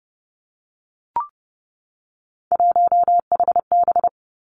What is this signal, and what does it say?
Morse code, heard as a keyed tone, about 30 wpm, 700 Hz. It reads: 1HB
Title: Three Challenging Letter-Number Combinations 30wpm